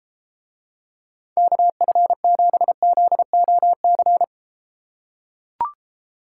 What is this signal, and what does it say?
Morse code, heard as a keyed tone, about 30 wpm, 700 Hz. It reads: KF7ZOC